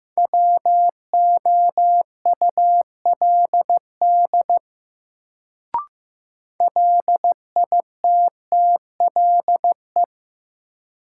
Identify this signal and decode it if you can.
Morse code, heard as a keyed tone, about 15 wpm, 700 Hz. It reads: WOULD LITTLE
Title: Sets of 2 Words - New Word Drill - 15wpm